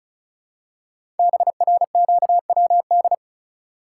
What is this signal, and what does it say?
Morse code, heard as a keyed tone, about 36 wpm, 700 Hz. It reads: BRQWD
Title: Code Groups - 5 Letters - 35wpm